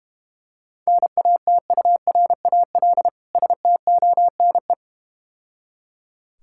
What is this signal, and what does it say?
Morse code, heard as a keyed tone, about 32 wpm, 690 Hz. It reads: NATURAL STONE